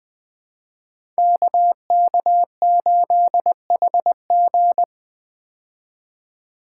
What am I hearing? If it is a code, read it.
Morse code, heard as a keyed tone, about 20 wpm, 700 Hz. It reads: KK8HG
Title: Code Groups - 5 Letter:Numbers - 20wpm